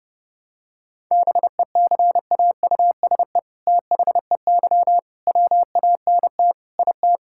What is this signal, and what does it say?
Morse code, heard as a keyed tone, about 30 wpm, 700 Hz. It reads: BECAUSE THEY WANT IT
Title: Sentences from Top 1000 Words - Review Part 5 - 30wpm